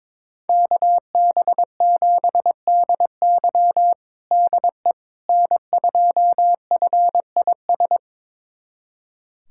Morse code, heard as a keyed tone, about 22 wpm, 700 Hz. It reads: KB7DY DE N2FIS
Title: USA Callsign DE Callsign - 5 Characters - 22wpm